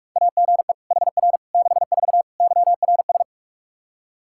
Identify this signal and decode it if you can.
Morse code, heard as a keyed tone, about 46 wpm, 700 Hz. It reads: AGE HR 64 YRS
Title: K1USN SST QSO Practice 45wpm